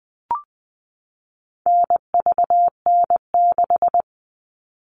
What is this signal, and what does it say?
Morse code, heard as a keyed tone, about 20 wpm, 700 Hz. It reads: NVN6
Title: All Cut Numbers - 4-Digit 20wpm